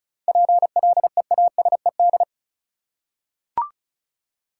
Morse code, heard as a keyed tone, about 36 wpm, 700 Hz. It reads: PLEASED
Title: Seven Letter Words - 35wpm